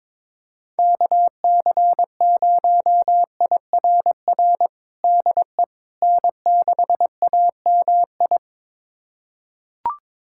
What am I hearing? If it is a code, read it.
Morse code, heard as a keyed tone, about 22 wpm, 700 Hz. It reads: KC0IRR DE N6AMI
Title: USA Callsign DE Callsign 22wpm